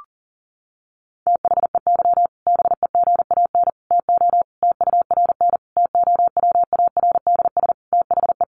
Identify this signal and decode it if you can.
Morse code, heard as a keyed tone, about 40 wpm, 700 Hz. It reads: THEY BEGAN TO TURN TOWARDS THE
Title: Sentences from Top 1100 Words 40wpm